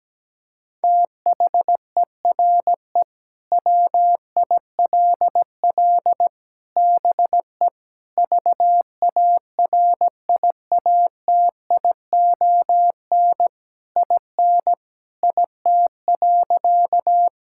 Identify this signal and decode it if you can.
Morse code, heard as a keyed tone, about 17 wpm, 700 Hz. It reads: THERE WILL BE VARIATION IN IT.
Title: Sentences from Top 1600 Words 17wpm